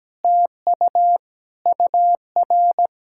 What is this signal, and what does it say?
Morse code, heard as a keyed tone, about 17 wpm, 700 Hz. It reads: TU UR
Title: SOTA QSO Practice 17wpm